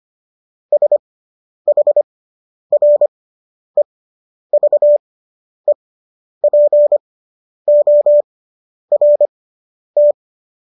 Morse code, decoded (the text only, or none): SHREVEPORT